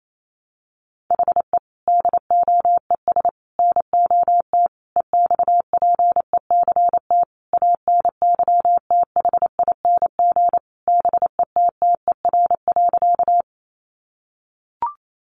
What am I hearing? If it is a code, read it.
Morse code, heard as a keyed tone, about 28 wpm, 700 Hz. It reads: HE DOES NOT EXPECT ANYTHING BETTER.
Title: Sentences from Top 1100 Words 28wpm